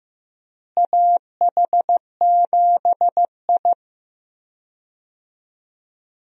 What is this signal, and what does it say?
Morse code, heard as a keyed tone, about 16 wpm, 700 Hz. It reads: AH7I
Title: USA Call Signs - 4 Characters - Repeated 3x - 15wpm